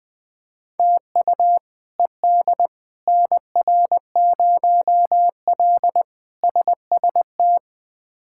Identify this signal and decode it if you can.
Morse code, heard as a keyed tone, about 20 wpm, 700 Hz. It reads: TU ED NR0L SST
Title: K1USN SST QSO Practice - 20wpm